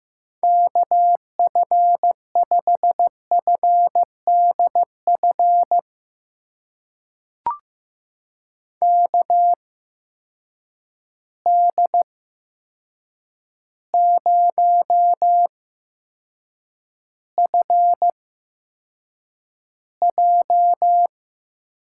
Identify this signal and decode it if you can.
Morse code, heard as a keyed tone, about 15 wpm, spaced as for 7 wpm, 700 Hz. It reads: KF5FDF KD0FJ
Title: USA Call Signs - 6 Characters - 8x Character Spacing - 15wpm